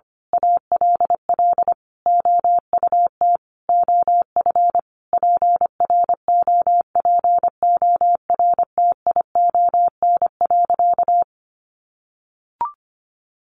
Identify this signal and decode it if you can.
Morse code, heard as a keyed tone, about 25 wpm, 700 Hz. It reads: ALL OUT OF PROPORTION.
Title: A Sound Of Thunder 25wpm